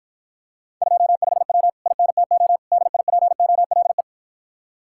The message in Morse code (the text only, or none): 25W INTO DIPOLE